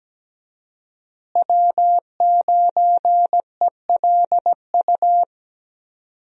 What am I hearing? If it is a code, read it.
Morse code, heard as a keyed tone, about 17 wpm, 700 Hz. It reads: W9ELU
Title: USA Call Signs - 5 Characters - Repeated 2x - 17wpm